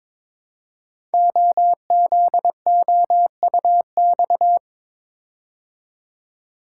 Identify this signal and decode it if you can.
Morse code, heard as a keyed tone, about 22 wpm, 700 Hz. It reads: OZOUX
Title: Code Groups - 5 Letter:Numbers - 22wpm